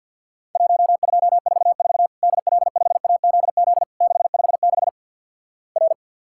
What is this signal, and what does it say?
Morse code, heard as a keyed tone, about 50 wpm, 700 Hz. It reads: 1244 DL5AZZ 656 R
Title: Worked All Europe DX Contest - QTC Exchange - 50wpm